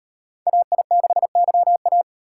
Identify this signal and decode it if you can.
Morse code, heard as a keyed tone, about 38 wpm, 700 Hz. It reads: AI6YA